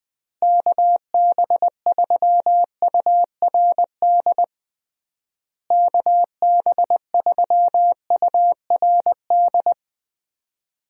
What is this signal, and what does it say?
Morse code, heard as a keyed tone, about 19 wpm, 700 Hz. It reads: KB3URD KB3URD
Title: USA Call Signs - 6 Characters - Repeated 3x - 20wpm